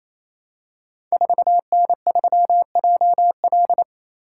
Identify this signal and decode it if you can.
Morse code, heard as a keyed tone, about 29 wpm, 700 Hz. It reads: 4N3JL
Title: Code Groups - 5 Letter:Numbers - 28wpm